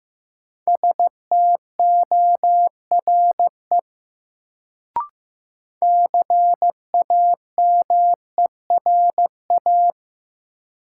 Morse code, heard as a keyed tone, about 15 wpm, 700 Hz. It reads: STORE CAMERA
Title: Sentences from Top 1000 Words - Review Part 5 - New Word Drill - 15wpm